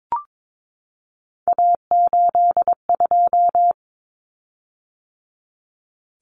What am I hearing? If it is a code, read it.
Morse code, heard as a keyed tone, about 22 wpm, 700 Hz. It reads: A82
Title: Common Cut Numbers - 3-Digit 22wpm